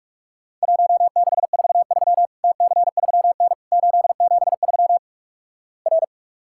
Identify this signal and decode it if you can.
Morse code, heard as a keyed tone, about 46 wpm, 700 Hz. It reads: 1643 TC3N 873 R
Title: Worked All Europe DX Contest - QTC Exchange - 45wpm